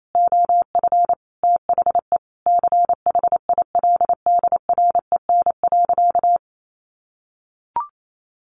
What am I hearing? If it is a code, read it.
Morse code, heard as a keyed tone, about 28 wpm, 700 Hz. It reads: OF THE CHILDREN.
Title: Sentences from Top 1200 Words 28wpm